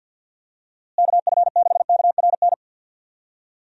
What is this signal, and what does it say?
Morse code, heard as a keyed tone, about 47 wpm, 700 Hz. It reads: KF6KRN
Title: USA Call Signs - 6 Characters - Repeated 2x - 50wpm